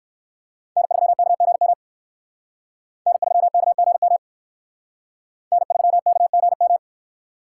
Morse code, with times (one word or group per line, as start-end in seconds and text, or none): 0.8-1.7 N3KGM
3.1-4.2 N3KGM
5.5-6.8 N3KGM